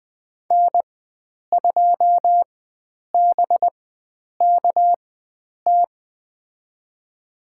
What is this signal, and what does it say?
Morse code, heard as a keyed tone, about 20 wpm, spaced as for 9 wpm, 700 Hz. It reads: N2BKT
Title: USA Call Signs - 5 Characters - 4x Character Spacing - 20wpm